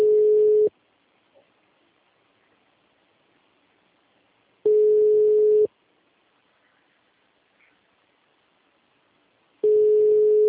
Ringback tone (Europe, 425 Hz).
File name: out- -2006- - - .39983